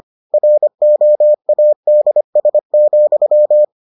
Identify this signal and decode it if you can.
Morse code, heard as a keyed tone, about 25 wpm, 600 Hz. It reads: ROADS,